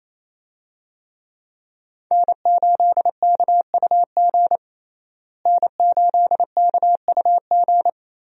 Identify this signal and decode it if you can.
Morse code, heard as a keyed tone, about 28 wpm, 700 Hz. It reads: N8KUG N8KUG
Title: USA Call Signs - 5 Characters - Repeated 3x - 28wpm